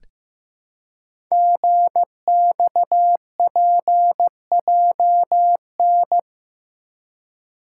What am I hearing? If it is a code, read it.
Morse code, heard as a keyed tone, about 15 wpm, 700 Hz. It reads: GXPJN
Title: Code Groups - 5 Letter:Numbers - 15wpm